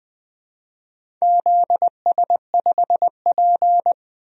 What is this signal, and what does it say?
Morse code, heard as a keyed tone, about 20 wpm, 700 Hz. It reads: ZS5P
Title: DXCC-100 Call Signs - 4 Characters - Repeated 2x - 20wpm